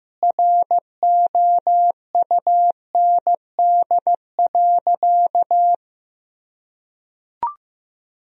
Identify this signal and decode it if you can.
Morse code, heard as a keyed tone, about 16 wpm, 700 Hz. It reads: ROUND.
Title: Sentences from Top 1000 Words - Review Part 3 - 15wpm